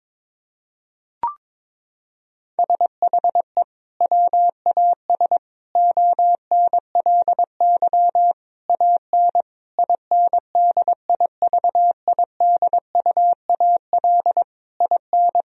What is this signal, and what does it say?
Morse code, heard as a keyed tone, about 22 wpm, 700 Hz. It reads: SHE WAS ONLY AN INDIVIDUAL IN
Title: Sentences from Top 1000 Words - Review Part 4 - 22wpm